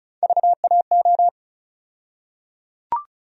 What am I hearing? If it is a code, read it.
Morse code, heard as a keyed tone, about 34 wpm, 700 Hz. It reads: VAO
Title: USA Call Sign Postfixes - 3 Letters 35wpm